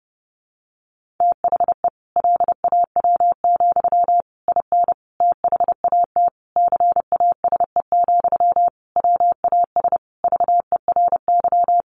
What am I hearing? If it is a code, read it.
Morse code, heard as a keyed tone, about 30 wpm, 700 Hz. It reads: THE LAW, IN THAT CASE, WAS VERY
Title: Sentences from Top 1400 Words 30wpm